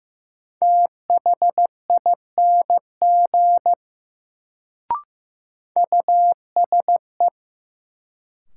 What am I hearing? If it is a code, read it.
Morse code, heard as a keyed tone, about 15 wpm, 700 Hz. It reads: THING USE
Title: Sentences from Top 100 Words - New Word Drill - 15wpm